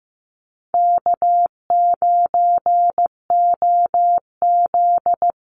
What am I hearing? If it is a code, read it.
Morse code, heard as a keyed tone, about 15 wpm, 700 Hz. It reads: K9OZ
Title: NA SOTA Chaser Callsigns 15wpm